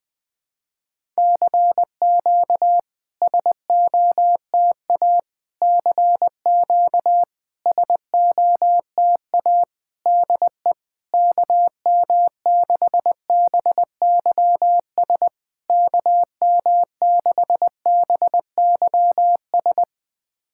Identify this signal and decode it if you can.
Morse code, heard as a keyed tone, about 20 wpm, 700 Hz. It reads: CQ SOTA CQ SOTA DE KM6BYS KM6BYS
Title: SOTA QSO Practice 20wpm